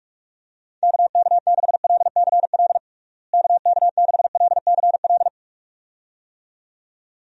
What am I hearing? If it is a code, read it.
Morse code, heard as a keyed tone, about 46 wpm, 700 Hz. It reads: KK6LCL KK6LCL
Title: USA Call Signs - 6 Characters - Repeated 2x - 45wpm